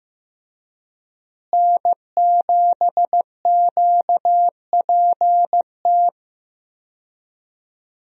Morse code, heard as a keyed tone, about 15 wpm, 700 Hz. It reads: N7QPT
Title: USA Call Signs - 5 Characters - Repeated 3x - 15wpm